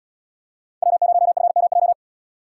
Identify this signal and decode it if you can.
Morse code, heard as a keyed tone, about 71 wpm, 700 Hz. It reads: W0FMJ